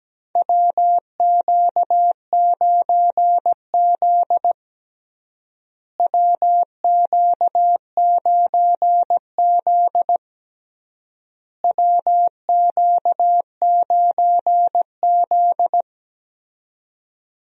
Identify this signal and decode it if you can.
Morse code, heard as a keyed tone, about 17 wpm, 700 Hz. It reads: WQ9Z WQ9Z WQ9Z
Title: USA Call Signs - 4 Characters - Repeated 3x - 17wpm